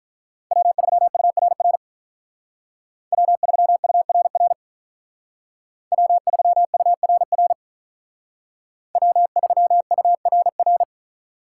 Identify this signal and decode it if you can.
Morse code, heard as a keyed tone, about 42 wpm, 700 Hz. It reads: W3URR W3URR W3URR W3URR
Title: Speed Racing - USA Call Signs 35wpm